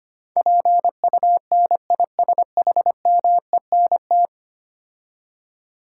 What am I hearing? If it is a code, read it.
Morse code, heard as a keyed tone, about 25 wpm, 690 Hz. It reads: PUNISHMENT
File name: 10 Letter Words - First 3 Letters Given - 25wpm